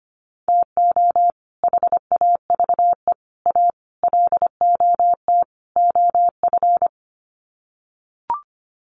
Morse code, heard as a keyed tone, about 25 wpm, 700 Hz. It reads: TO HAVE A LOT OF
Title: Sets of 5 Words Encore 25wpm